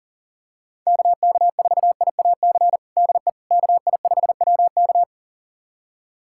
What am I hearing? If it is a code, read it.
Morse code, heard as a keyed tone, about 40 wpm, 700 Hz. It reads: KK4IAC DE KI5WK
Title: USA Callsign DE Callsign 40wpm